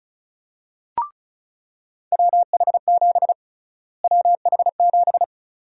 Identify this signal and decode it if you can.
Morse code, heard as a keyed tone, about 36 wpm, 700 Hz. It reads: WH7 WH7
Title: USA Call Sign Prefixes - Repeated 2x - 35wpm